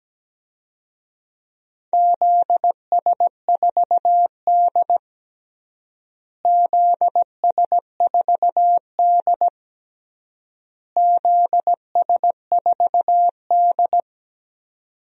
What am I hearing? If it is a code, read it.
Morse code, heard as a keyed tone, about 17 wpm, 700 Hz. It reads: ZS4D ZS4D ZS4D
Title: DXCC-100 Call Signs - 4 Characters - Repeated 3x - 17wpm